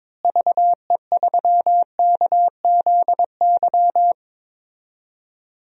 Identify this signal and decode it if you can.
Morse code, heard as a keyed tone, about 21 wpm, 700 Hz. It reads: VE3KZY
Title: DXCC-100 Call Signs - 6 Characters - Repeated 2x - 22wpm